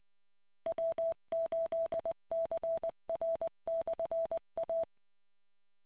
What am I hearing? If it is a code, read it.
Morse code, heard as a keyed tone, about 20 wpm, 660 Hz. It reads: W8CR/A